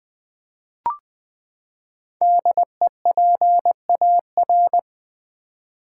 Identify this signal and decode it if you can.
Morse code, heard as a keyed tone, about 20 wpm, 700 Hz. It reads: DEPAR
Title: Build A Word - 5 Letters At A Time - 10-Letter Words - 20wpm